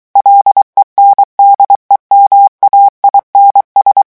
Morse code, heard as a keyed tone, about 23 wpm, 800 Hz. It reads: LENDEMAINS
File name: morse mp3 / revision9 citations humoristiques vitesse 22 mots minute